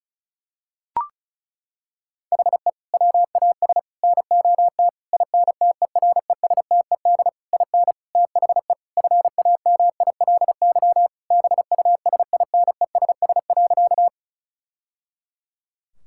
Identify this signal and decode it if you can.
Morse code, heard as a keyed tone, about 36 wpm, 700 Hz. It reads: HE WAS NOT INTERESTED IN THE FAMILY BUSINESS.
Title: Sentences from Top 1000 Words - Review Part 6 - 35wpm